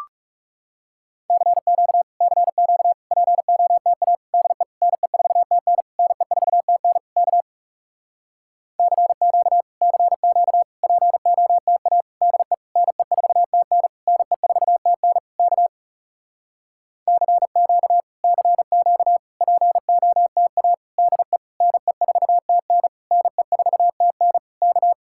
Morse code, heard as a keyed tone, about 40 wpm, 700 Hz. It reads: CQ CQ POTA DE NE4TN NE4TN K CQ CQ POTA DE NE4TN NE4TN K CQ CQ POTA DE NE4TN NE4TN K